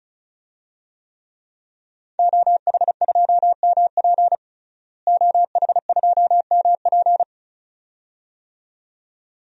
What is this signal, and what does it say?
Morse code, heard as a keyed tone, about 36 wpm, 700 Hz. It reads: OH2MP OH2MP
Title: DXCC-100 Call Signs - 5 Characters - Repeated 2x - 35wpm